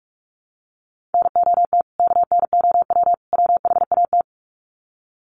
Morse code, heard as a keyed tone, about 44 wpm, 700 Hz. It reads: NOT KNOW WHAT
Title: Sets of 3 Words 45wpm